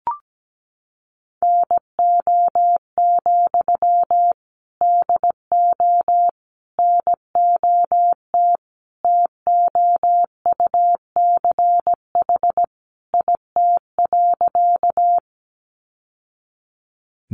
Morse code, heard as a keyed tone, about 17 wpm, 700 Hz. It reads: NO, DO NOT TOUCH IT.
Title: Sentences from Top 1200 Words 17wpm